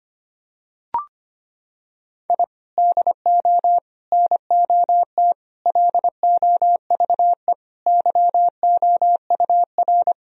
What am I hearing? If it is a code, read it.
Morse code, heard as a keyed tone, about 25 wpm, 700 Hz. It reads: I DO NOT LOVE YOUR
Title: Sentences from Top 1000 Words - Review Part 4 - 25wpm